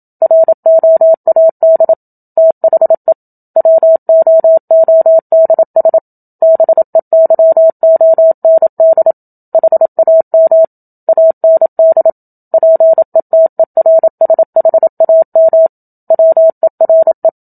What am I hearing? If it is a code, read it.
Morse code, heard as a keyed tone, about 27 wpm, 650 Hz. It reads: ROAD THE WOODS BEYOND HAM AND PETERSHAM WERE
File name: Book Two / 01-Chapter 1 - 27 WPM